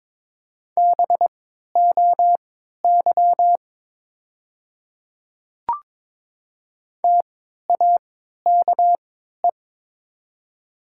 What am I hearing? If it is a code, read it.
Morse code, heard as a keyed tone, about 22 wpm, spaced as for 12 wpm, 700 Hz. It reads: BOY TAKE
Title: 3x Character Spacing - Top 200 Words - 22wpm